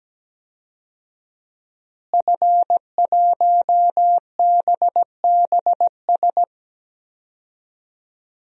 Morse code, heard as a keyed tone, about 17 wpm, 700 Hz. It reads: F1BBS
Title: DXCC-100 Call Signs - 5 Characters - Repeated 3x - 17wpm